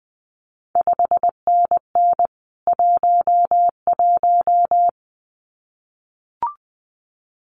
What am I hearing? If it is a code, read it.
Morse code, heard as a keyed tone, about 20 wpm, 700 Hz. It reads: 5NN 11
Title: CQ WW DX Contest Exchange Encore - 20wpm